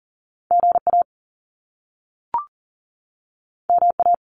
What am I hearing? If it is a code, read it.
Morse code, heard as a keyed tone, about 38 wpm, 700 Hz. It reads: GA MA
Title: US States 40wpm